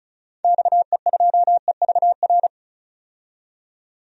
Morse code, heard as a keyed tone, about 34 wpm, 700 Hz. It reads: XE2EVR
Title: DXCC-100 Call Signs - 6 Characters - 35wpm